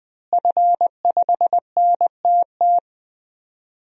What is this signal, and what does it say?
Morse code, heard as a keyed tone, about 20 wpm, 700 Hz. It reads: F5NTT
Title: DXCC-100 Call Signs - 5 Characters - Repeated 2x - 20wpm